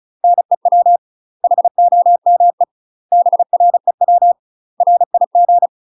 Morse code, heard as a keyed tone, about 36 wpm, 700 Hz. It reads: NEW HOME BREW RIG
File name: UK MASTER QSOs TRAINING / 35WPM / UK MASTER QSOs 0012